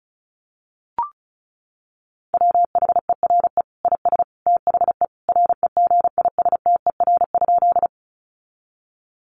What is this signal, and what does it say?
Morse code, heard as a keyed tone, about 34 wpm, 700 Hz. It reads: WHERE IS THE REGISTER?
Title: Sentences from Top 1200 Words 35wpm